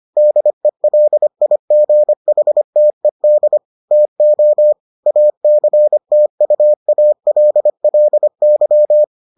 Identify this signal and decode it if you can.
Morse code, heard as a keyed tone, about 25 wpm, 600 Hz. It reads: DELIGHTED TO ACTUALLY